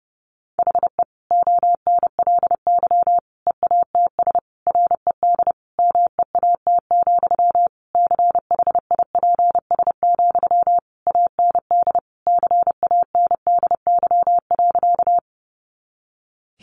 Morse code, heard as a keyed tone, about 30 wpm, 700 Hz. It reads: HE ONLY EATS RED MEAT, CHIPS, AND CANDY.
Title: Easy Phrases 30wpm